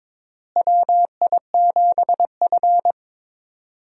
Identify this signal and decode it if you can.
Morse code, heard as a keyed tone, about 22 wpm, 700 Hz. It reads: WI7F
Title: USA Call Signs - 4 Characters - Repeated 3x - 22wpm